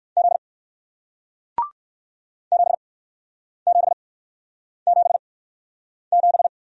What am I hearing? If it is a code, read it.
Morse code, heard as a keyed tone, about 59 wpm, spaced as for 9 wpm, 700 Hz. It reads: Z 7777